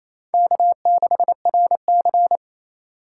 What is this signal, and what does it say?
Morse code, heard as a keyed tone, about 27 wpm, 700 Hz. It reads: K6RC
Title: K1USN SST QSO Practice - 28wpm